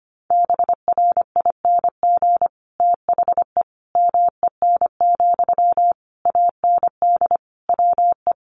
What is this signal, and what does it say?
Morse code, heard as a keyed tone, about 25 wpm, 700 Hz. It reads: BRING THE MEN, AND WE
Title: Sentences from Top 1400 Words 25wpm